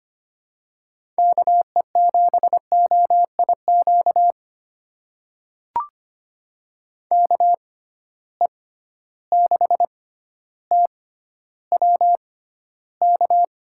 Morse code, heard as a keyed tone, about 25 wpm, spaced as for 13 wpm, 700 Hz. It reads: KE7OIQ KE6TWK
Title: USA Call Signs - 6 Characters - 6x Character Spacing - 25wpm